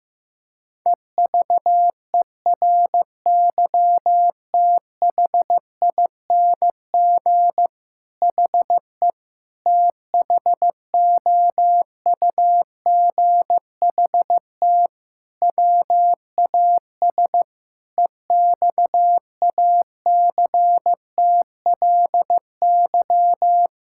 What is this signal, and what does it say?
Morse code, heard as a keyed tone, about 15 wpm, 700 Hz. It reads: EVERYTHING HE THOUGHT WAS EXACTLY
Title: Sentences from Top 1000 Words - Review Part 6 - 15wpm